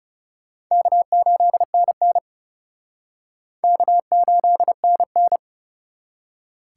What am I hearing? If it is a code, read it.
Morse code, heard as a keyed tone, about 36 wpm, 700 Hz. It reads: K8NN K8NN
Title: Speed Racing - USA 1x2 Call Signs 30wpm